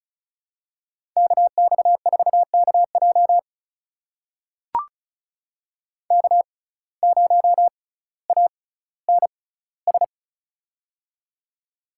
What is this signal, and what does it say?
Morse code, heard as a keyed tone, about 36 wpm, spaced as for 18 wpm, 700 Hz. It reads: KX4KJ K0ANS
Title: USA Call Signs - 5 Characters - 6x Character Spacing - 35wpm